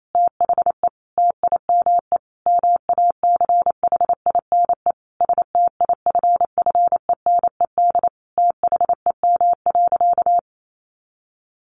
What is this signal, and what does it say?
Morse code, heard as a keyed tone, about 28 wpm, 700 Hz. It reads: THE TIME MACHINE STIFFENED THEM.
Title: A Sound Of Thunder 28wpm